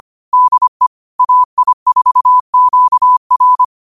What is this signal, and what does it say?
Morse code, heard as a keyed tone, about 25 wpm, 1000 Hz. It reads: DE AI4QR
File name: qotdfiles / 25wpm